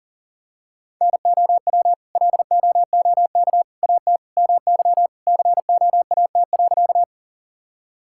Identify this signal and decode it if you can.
Morse code, heard as a keyed tone, about 40 wpm, 700 Hz. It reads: NOW LOOK AT MY COAT.
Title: Sentences from Top 1600 Words 40wpm